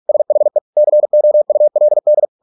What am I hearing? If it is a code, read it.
Morse code, heard as a keyed tone, about 46 wpm, 600 Hz. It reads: SHE COULD